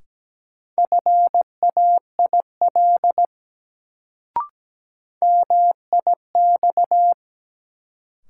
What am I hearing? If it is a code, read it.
Morse code, heard as a keyed tone, about 17 wpm, 700 Hz. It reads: FAIL MIX
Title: Sentences from Top 1100 Words - New Word Drill Alt - 17wpm